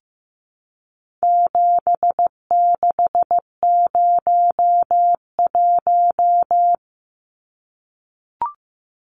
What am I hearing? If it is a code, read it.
Morse code, heard as a keyed tone, about 15 wpm, 700 Hz. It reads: 7601
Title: Numbers - 4-Digit 15wpm